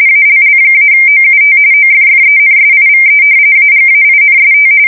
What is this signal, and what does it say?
RTTY, about 45.45 baud. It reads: ALL-OUT OFFENSIVE AGAINST
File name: fsk170 / battleshipnj